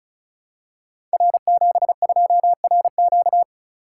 Morse code, heard as a keyed tone, about 36 wpm, 700 Hz. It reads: R72RQ
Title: Code Groups - 5 Letter:Numbers - 35wpm